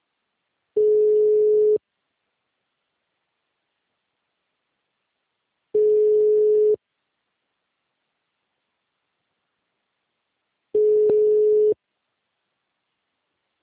Ringback tone (Europe, 425 Hz).
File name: out- -2005- - - .46370